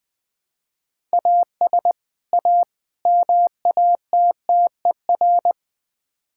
Morse code, heard as a keyed tone, about 20 wpm, 700 Hz. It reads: AS A MATTER
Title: Build A Phrase - 5-Word Phrases - 20wpm